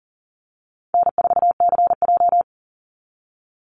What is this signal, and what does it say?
Morse code, heard as a keyed tone, about 40 wpm, 700 Hz. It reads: N4CJ
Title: Easy Call Signs - 40wpm